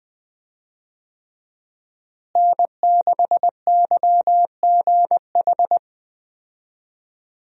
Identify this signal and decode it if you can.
Morse code, heard as a keyed tone, about 20 wpm, 700 Hz. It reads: N6YGH
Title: USA Call Signs - 5 Characters - Repeated 3x - 20wpm